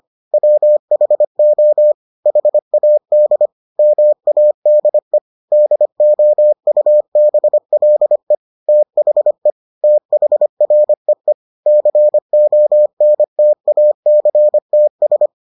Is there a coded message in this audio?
Yes: Morse code from 0.3 to 15.3 s, heard as a keyed tone, about 25 wpm, 600 Hz. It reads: WHO HAD MADE DOUBLE THE THREE CONTACTS